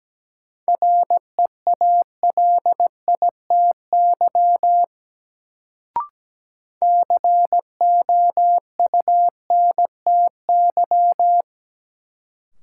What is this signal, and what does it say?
Morse code, heard as a keyed tone, about 17 wpm, 700 Hz. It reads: REALITY COUNTY
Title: Sentences from Top 1200 Words - New Word Drill - 17wpm